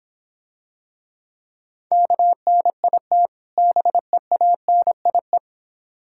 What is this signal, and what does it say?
Morse code, heard as a keyed tone, about 26 wpm, 700 Hz. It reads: KNIT BEANIE